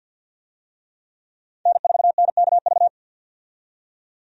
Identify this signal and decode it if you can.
Morse code, heard as a keyed tone, about 53 wpm, 700 Hz. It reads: N4NKV
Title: USA Call Signs - 5 Characters - 50wpm